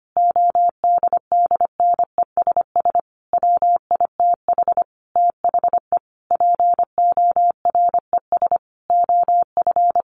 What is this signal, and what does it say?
Morse code, heard as a keyed tone, about 25 wpm, 700 Hz. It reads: ODDNESS WITH THE PORES OF
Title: A Sound Of Thunder 25wpm